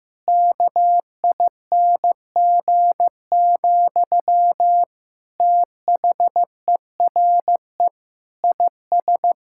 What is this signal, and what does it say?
Morse code, heard as a keyed tone, about 15 wpm, 700 Hz. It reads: KING, THERE IS
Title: Sentences from Top 1000 Words - Review Part 6 - 15wpm